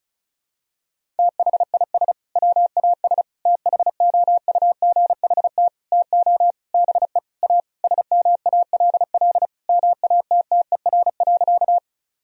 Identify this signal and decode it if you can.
Morse code, heard as a keyed tone, about 36 wpm, 700 Hz. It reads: THIS WAS THOUGHT TO BE A SMALL MATTER.
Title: Sentences from Top 1000 Words - Review Part 1 - 35wpm